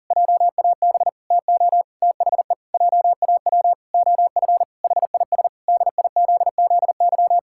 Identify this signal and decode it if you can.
Morse code, heard as a keyed tone, about 40 wpm, 700 Hz. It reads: JAB TO THE JAW OF HIS DIZZY
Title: Pangram Challenge v2 - 40wpm